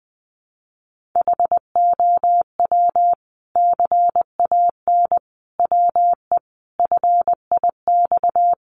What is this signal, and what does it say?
Morse code, heard as a keyed tone, about 20 wpm, 700 Hz. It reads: HOW CAN WE FIX
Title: Sentences from Top 1400 Words 20wpm